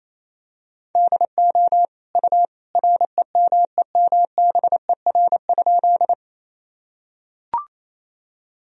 Morse code, heard as a keyed tone, about 27 wpm, 700 Hz. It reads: DO U REMEMBER?
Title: Sentences from Top 400 Words and Abbreviations 28wpm